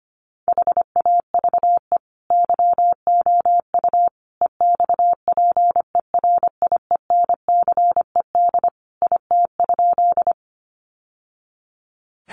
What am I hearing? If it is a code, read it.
Morse code, heard as a keyed tone, about 25 wpm, 700 Hz. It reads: HAVE YOU EXPERIENCED IT?
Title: Sentences from Top 1100 Words 25wpm